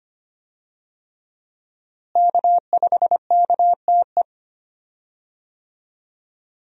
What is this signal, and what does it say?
Morse code, heard as a keyed tone, about 25 wpm, 700 Hz. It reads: K5KTE
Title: USA Call Signs - 5 Characters - Repeated 3x - 25wpm